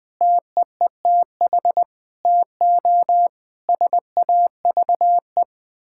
Morse code, heard as a keyed tone, about 20 wpm, 700 Hz. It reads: TEETH TO SAVE
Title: Difficult Sentences with Hints - 20wpm